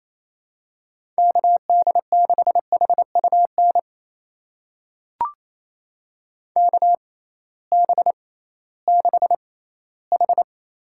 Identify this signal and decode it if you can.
Morse code, heard as a keyed tone, about 28 wpm, spaced as for 16 wpm, 700 Hz. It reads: KD6HUN KB6H
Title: USA Call Signs - 6 Characters - 6x Character Spacing - 28wpm